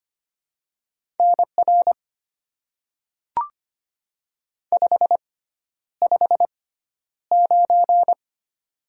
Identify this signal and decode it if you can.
Morse code, heard as a keyed tone, about 25 wpm, spaced as for 11 wpm, 700 Hz. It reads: NR 559
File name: Top 90 QSO Elements - 6x Character Spacing - 25wpm